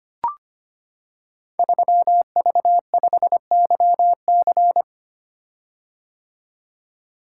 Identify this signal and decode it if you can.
Morse code, heard as a keyed tone, about 25 wpm, 700 Hz. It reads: 3V5YC
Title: Code Groups - 5 Letter:Numbers - 25wpm